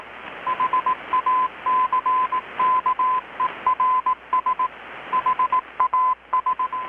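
Morse code, heard as a keyed tone, about 18 wpm, 1020 Hz. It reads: HACKERS HAS
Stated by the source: Morse from KPH's Night of Nights XX maritime radiotelegraphy event, received in Chiapas, Mexico about 3,800 km away on a Kenwood R-600 with a grounded 10 m long-wire antenna.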